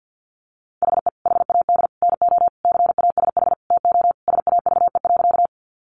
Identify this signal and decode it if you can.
Morse code, heard as a keyed tone, about 48 wpm, 700 Hz. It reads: HE HAD NO CASH TO SAVE.
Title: Sentences from Top 1100 Words 50wpm